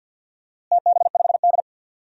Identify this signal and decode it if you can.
Morse code, heard as a keyed tone, about 47 wpm, 700 Hz. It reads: TB5D
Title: DXCC-100 Call Signs - 4 Characters - Repeated 3x - 50wpm